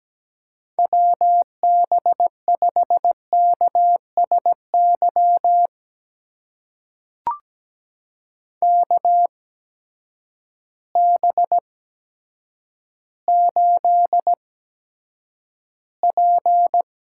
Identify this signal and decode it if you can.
Morse code, heard as a keyed tone, about 17 wpm, spaced as for 8 wpm, 700 Hz. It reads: WB5KSY KB8P
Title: USA Call Signs - 6 Characters - 8x Character Spacing - 17wpm